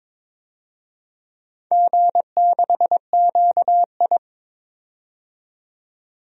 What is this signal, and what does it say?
Morse code, heard as a keyed tone, about 22 wpm, 700 Hz. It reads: G6QI
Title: DXCC-100 Call Signs - 4 Characters - 22wpm